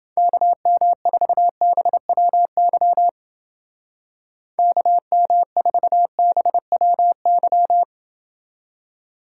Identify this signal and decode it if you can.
Morse code, heard as a keyed tone, about 29 wpm, 700 Hz. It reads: KM4BWY KM4BWY